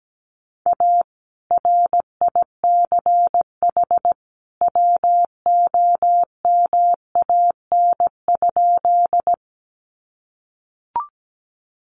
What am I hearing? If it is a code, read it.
Morse code, heard as a keyed tone, about 17 wpm, 700 Hz. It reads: A RICH WOMAN?
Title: Sentences from Top 1100 Words 17wpm